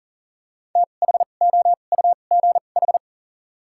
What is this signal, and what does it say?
Morse code, heard as a keyed tone, about 40 wpm, spaced as for 29 wpm, 690 Hz. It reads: THOUGH
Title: 2x Character Spacing - Top 500 Words - 40wpm